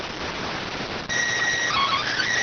SSTV (slow-scan television), PD 290.